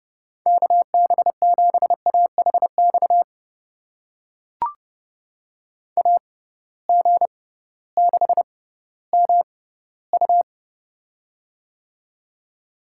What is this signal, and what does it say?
Morse code, heard as a keyed tone, about 29 wpm, spaced as for 16 wpm, 700 Hz. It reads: KB7AHX AG6MU
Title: USA Call Signs - 6x Character Spacing - 30wpm